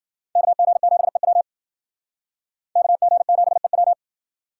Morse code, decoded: KG7EW KG7EW

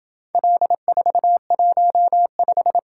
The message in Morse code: L415

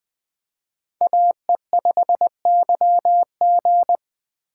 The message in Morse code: AE5YG

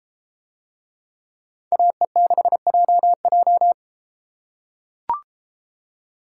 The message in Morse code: AE6JJ

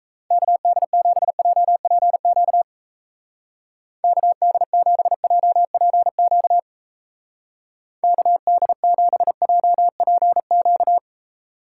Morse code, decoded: KD7JPQ KD7JPQ KD7JPQ

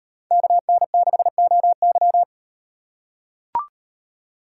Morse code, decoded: KN6OY

